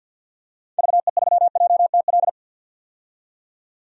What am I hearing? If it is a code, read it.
Morse code, heard as a keyed tone, about 52 wpm, 700 Hz. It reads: VE3JTL
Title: DXCC-100 Call Signs - 6 Characters - 50wpm